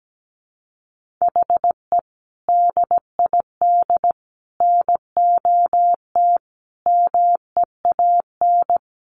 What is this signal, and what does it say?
Morse code, heard as a keyed tone, about 17 wpm, 700 Hz. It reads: HE DID NOT MEAN
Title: Sentences from Top 1500 Words 17wpm